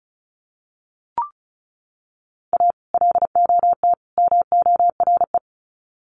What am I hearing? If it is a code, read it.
Morse code, heard as a keyed tone, about 34 wpm, 700 Hz. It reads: A LOT MORE
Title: Sets of 3 Words 35wpm